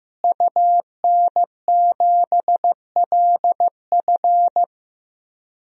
Morse code, heard as a keyed tone, about 15 wpm, 700 Hz. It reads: UN7LF
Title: DXCC-100 Call Signs - 5 Characters - Repeated 3x - 15wpm